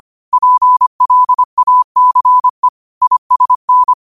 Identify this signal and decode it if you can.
Morse code, heard as a keyed tone, about 25 wpm, 1000 Hz. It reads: PLACE ISN
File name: qotdfiles / 25wpm